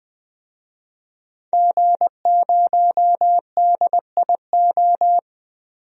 Morse code, heard as a keyed tone, about 21 wpm, 700 Hz. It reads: G0DIO
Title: DXCC-100 Call Signs - 5 Characters - 20wpm